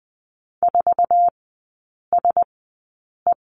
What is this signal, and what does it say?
Morse code, heard as a keyed tone, about 20 wpm, spaced as for 8 wpm, 700 Hz. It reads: 4SE
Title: 047 Lesson 9e - Number 4 - Three Characters 20wpm